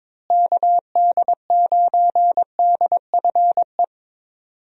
Morse code, heard as a keyed tone, about 22 wpm, 700 Hz. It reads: KD9DFE
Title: USA Call Signs - 6 Characters - Repeated 2x - 22wpm